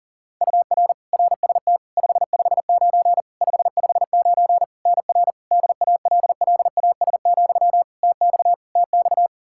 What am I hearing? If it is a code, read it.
Morse code, heard as a keyed tone, about 40 wpm, 700 Hz. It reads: UR RST 559 559 NR DALLAS, TX TX
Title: Practice QSO - Exchange 1 - Beginning - 40wpm